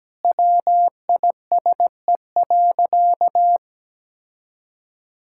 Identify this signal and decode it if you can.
Morse code, heard as a keyed tone, about 17 wpm, 700 Hz. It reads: WISE.
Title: Sentences from Top 1600 Words 17wpm